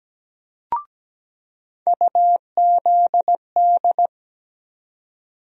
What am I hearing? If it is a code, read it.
Morse code, heard as a keyed tone, about 17 wpm, 700 Hz. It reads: UZD
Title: USA Call Sign Postfixes - 3 Letters 17wpm